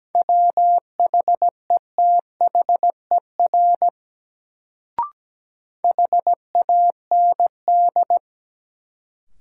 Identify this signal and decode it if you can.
Morse code, heard as a keyed tone, about 17 wpm, 700 Hz. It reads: WHETHER HAND
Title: Sets of 3 Words - New Word Drill - 17wpm